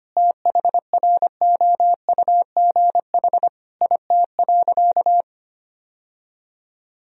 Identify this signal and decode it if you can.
Morse code, heard as a keyed tone, about 25 wpm, 700 Hz. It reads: THROUGH IT.
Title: Sentences from Top 1000 Words - Review Part 1 - 25wpm